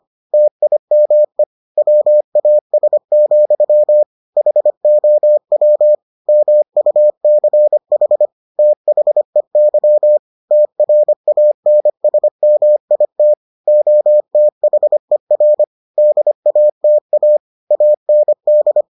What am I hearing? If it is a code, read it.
Morse code, heard as a keyed tone, about 25 wpm, 600 Hz. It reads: TIME WAS, HOW MUCH THEY TRANSMIT OTHER DATA AND